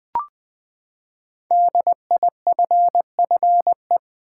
Morse code, heard as a keyed tone, about 20 wpm, 700 Hz. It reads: DIFFE
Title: Build A Word - 5 Letters At A Time - 10-Letter Words - 20wpm